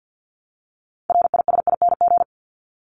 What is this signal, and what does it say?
Morse code, heard as a keyed tone, about 50 wpm, 700 Hz. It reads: RISING
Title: ING Suffix - Top 100 Words with Base Words 50wpm